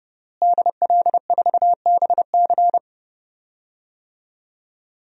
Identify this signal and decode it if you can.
Morse code, heard as a keyed tone, about 30 wpm, 700 Hz. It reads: DL4BC
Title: DXCC-100 Call Signs - 5 Characters - Repeated 2x - 30wpm